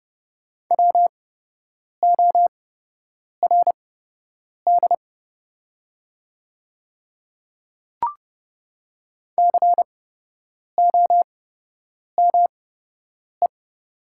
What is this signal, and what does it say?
Morse code, heard as a keyed tone, about 30 wpm, spaced as for 8 wpm, 700 Hz. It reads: WORD COME
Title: 8x Character Spacing - Top 100 Words - 30wpm